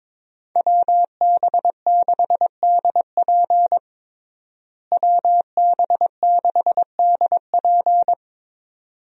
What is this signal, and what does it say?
Morse code, heard as a keyed tone, about 23 wpm, 700 Hz. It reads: WB6DP WB6DP
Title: USA Call Signs - 5 Characters - Repeated 2x - 22wpm